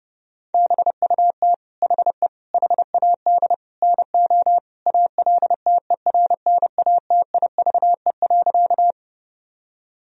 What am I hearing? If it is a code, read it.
Morse code, heard as a keyed tone, about 30 wpm, 700 Hz. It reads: BUT HE HAD NO ALTERNATIVE.
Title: Sentences from Top 1000 Words - Review Part 4 - 30wpm